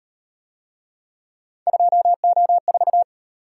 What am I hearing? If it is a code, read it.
Morse code, heard as a keyed tone, about 37 wpm, 700 Hz. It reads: 2O4